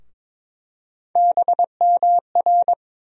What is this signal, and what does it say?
Morse code, heard as a keyed tone, about 22 wpm, 700 Hz. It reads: BMR
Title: USA Call Sign Postfixes - 3 Letters 22wpm